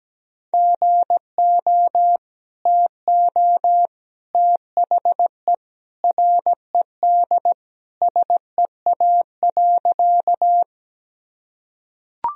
Morse code, heard as a keyed tone, about 17 wpm, 700 Hz. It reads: GO TO THE RED SEA.
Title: Sentences from Top 1000 Words - Review Part 2 - 17wpm